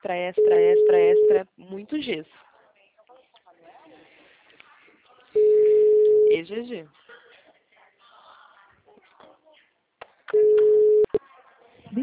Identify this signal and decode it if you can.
Ringback tone (Europe, 425 Hz), with speech.
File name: out- -2004- - - .16772